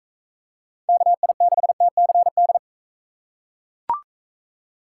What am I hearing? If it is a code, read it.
Morse code, heard as a keyed tone, about 39 wpm, 700 Hz. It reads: KI6TCD